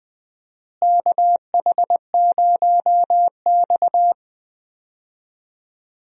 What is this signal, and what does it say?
Morse code, heard as a keyed tone, about 20 wpm, 700 Hz. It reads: KH0X
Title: USA Call Signs - 4 Characters - Repeated 2x - 20wpm